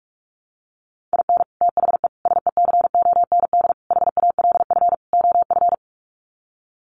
Morse code, heard as a keyed tone, about 44 wpm, 700 Hz. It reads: IN THE SECOND HALF OF
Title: Sets of 5 Words 45wpm